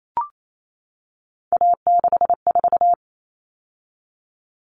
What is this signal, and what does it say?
Morse code, heard as a keyed tone, about 28 wpm, 700 Hz. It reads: A64
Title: All Cut Numbers - 3-Digit 28wpm